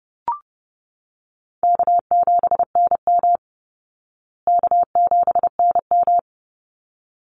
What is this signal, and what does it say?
Morse code, heard as a keyed tone, about 30 wpm, 700 Hz. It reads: K7NM K7NM
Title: Call Signs Repeated 3x - 30wpm